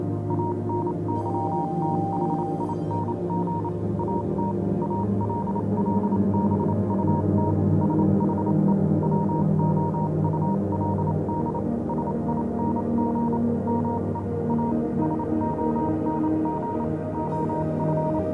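Morse code, heard as a keyed tone, about 28 wpm, spaced as for 25 wpm, 940 Hz, with noise. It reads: ANYTHING AT ALL BUT SHE WOULD SING ME A SONG OF
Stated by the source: Morse code version of a fan-made pony music track.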